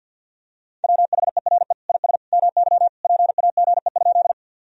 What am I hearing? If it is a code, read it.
Morse code, heard as a keyed tone, about 45 wpm, 700 Hz. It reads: WHERE IS MY PAGE?
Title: Sentences from Top 1000 Words - Review Part 1 - 50wpm